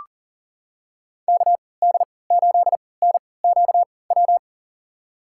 Morse code, heard as a keyed tone, about 40 wpm, spaced as for 23 wpm, 700 Hz. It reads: KD8NQW
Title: USA Call Signs - 6 Characters - 3x Character Spacing - 40wpm